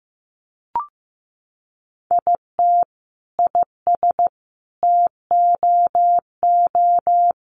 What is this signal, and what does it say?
Morse code, heard as a keyed tone, about 15 wpm, 700 Hz. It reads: IT IS TOO
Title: Sentences from Top 1100 Words 15wpm